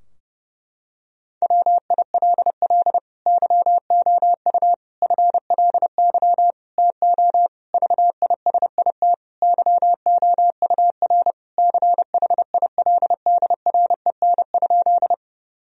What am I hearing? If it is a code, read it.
Morse code, heard as a keyed tone, about 32 wpm, 700 Hz. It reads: WILL YOU FLY TO VISIT YOUR CHILDREN?
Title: Sentences from Top 1000 Words - Review Part 2 - 30wpm